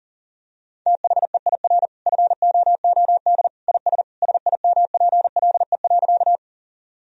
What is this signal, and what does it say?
Morse code, heard as a keyed tone, about 40 wpm, 700 Hz. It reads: THEIR FOOD IS SIMPLE.
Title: Sentences from Top 1000 Words - Review Part 1 - 40wpm